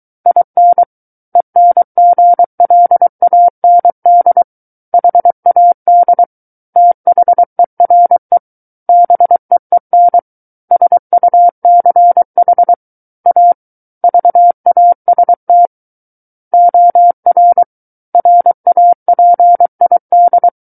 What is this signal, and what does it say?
Morse code, heard as a keyed tone, about 23 wpm, 700 Hz. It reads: IN ENGLAND HAD THERE BEEN SUCH A VAST OR RAPID
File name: Book One / 14-Chapter 14 - 23 WPM